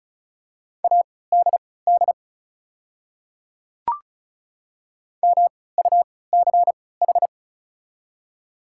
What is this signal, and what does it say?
Morse code, heard as a keyed tone, about 36 wpm, spaced as for 20 wpm, 700 Hz. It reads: ADD MUCH
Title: 3x Character Spacing - Top 300 Words - 35wpm